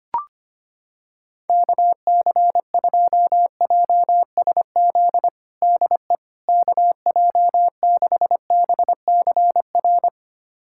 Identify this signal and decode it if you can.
Morse code, heard as a keyed tone, about 25 wpm, 700 Hz. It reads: KC2JSZ DE KJ6BCR
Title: USA Callsign DE Callsign - 6 Characters - 25wpm